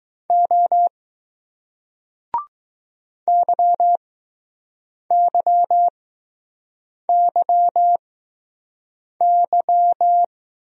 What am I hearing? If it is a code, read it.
Morse code, heard as a keyed tone, about 20 wpm, spaced as for 6 wpm, 700 Hz. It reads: O YYYY